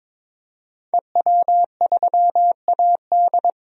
Morse code, heard as a keyed tone, about 22 wpm, 700 Hz. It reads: EW3AD
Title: DXCC-100 Call Signs - 5 Characters - 22wpm